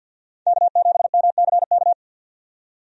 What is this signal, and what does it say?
Morse code, heard as a keyed tone, about 53 wpm, 700 Hz. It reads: K7MCK
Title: USA Call Signs - 5 Characters - 50wpm